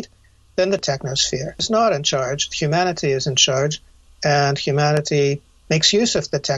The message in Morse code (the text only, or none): none